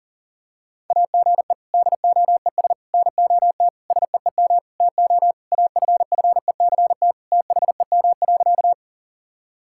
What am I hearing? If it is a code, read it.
Morse code, heard as a keyed tone, about 37 wpm, 700 Hz. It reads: AGE DOES NOT SEEM TO AFFECT THEM.
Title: Sentences from Top 1000 Words - Review Part 6 - 40wpm